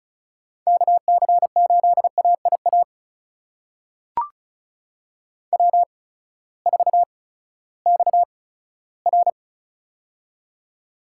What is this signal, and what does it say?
Morse code, heard as a keyed tone, about 36 wpm, spaced as for 18 wpm, 700 Hz. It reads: KC8AIA W4XR
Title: USA Call Signs - 8x Character Spacing - 35wpm